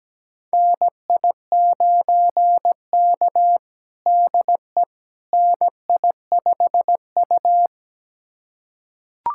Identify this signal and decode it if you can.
Morse code, heard as a keyed tone, about 17 wpm, 700 Hz. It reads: NI9K DE NI5U
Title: USA Callsign DE Callsign - 4 Characters - 17wpm